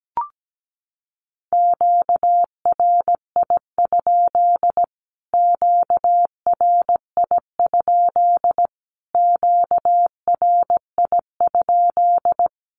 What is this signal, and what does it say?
Morse code, heard as a keyed tone, about 17 wpm, 700 Hz. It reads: QRI? QRI? QRI?
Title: Q-Codes Repeated 3x 17wpm